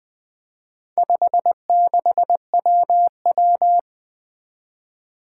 Morse code, heard as a keyed tone, about 20 wpm, 700 Hz. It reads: 56WW
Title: Four Challenging Letter-Number Combinations 20wpm